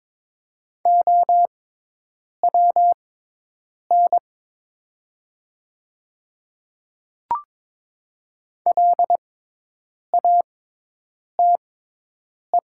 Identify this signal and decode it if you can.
Morse code, heard as a keyed tone, about 22 wpm, spaced as for 8 wpm, 700 Hz. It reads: OWN LATE
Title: 6x Character Spacing - Top 300 Words - 22wpm